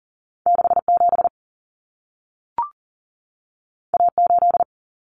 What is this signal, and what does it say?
Morse code, heard as a keyed tone, about 40 wpm, 700 Hz. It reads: 67 A8
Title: Common Cut Numbers - 2-Digit 40wpm